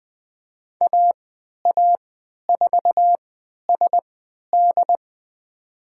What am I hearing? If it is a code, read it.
Morse code, heard as a keyed tone, about 20 wpm, spaced as for 11 wpm, 700 Hz. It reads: AA4SD
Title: USA Call Signs - 5 Characters - 3x Character Spacing - 20wpm